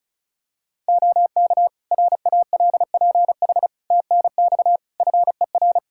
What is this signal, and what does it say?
Morse code, heard as a keyed tone, about 34 wpm, 700 Hz. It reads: OK RALPH TNX FER
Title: Practice QSO - Exchange 4 - Ending - 35wpm